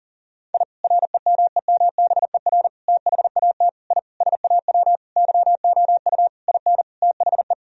Morse code, heard as a keyed tone, about 40 wpm, 700 Hz. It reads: I REMEMBER THAT I SAW YOU IN THE
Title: Sentences from Top 1000 Words - Review Part 1 - 40wpm